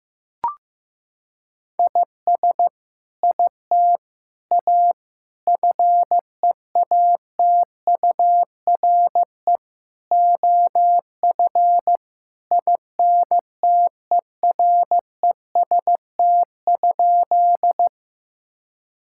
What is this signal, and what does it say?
Morse code, heard as a keyed tone, about 15 wpm, 700 Hz. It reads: IS IT A FEATURE OF INTEREST?
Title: Sentences from Top 1000 Words - Review Part 5 - 15wpm